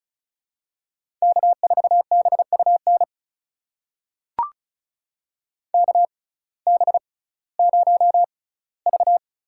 Morse code, heard as a keyed tone, about 34 wpm, spaced as for 19 wpm, 700 Hz. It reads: K4BUN KB0V
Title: USA Call Signs - 5 Characters - 6x Character Spacing - 35wpm